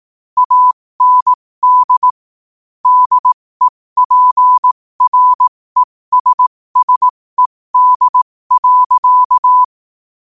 Morse code, heard as a keyed tone, about 18 wpm, spaced as for 15 wpm, 1000 Hz. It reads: AND DEPRESSED.
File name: qotdfiles / 15wpm